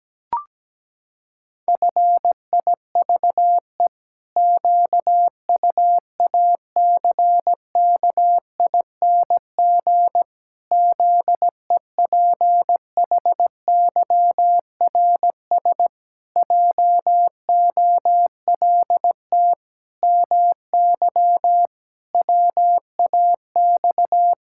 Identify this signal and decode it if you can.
Morse code, heard as a keyed tone, about 17 wpm, 700 Hz. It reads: FIVE QUACKING ZEPHYRS JOLT MY WAX
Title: Pangram Challenge v2 - 17wpm